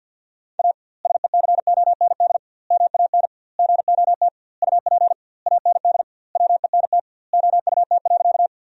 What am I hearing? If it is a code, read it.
Morse code, heard as a keyed tone, about 52 wpm, 700 Hz. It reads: A SECOND MAN GOT UP AND WENT OUT.
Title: 2x Word-Spacing - Sentences from Top 300 Words - 50wpm